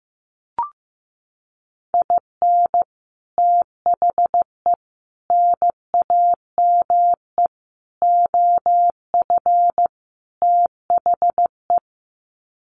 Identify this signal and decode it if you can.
Morse code, heard as a keyed tone, about 14 wpm, 700 Hz. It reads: IN THE NAME OF THE
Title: Sets of 5 Words Encore 15wpm